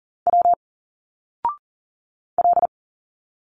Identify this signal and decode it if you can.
Morse code, heard as a keyed tone, about 40 wpm, 700 Hz. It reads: W L
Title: Single Letters 40wpm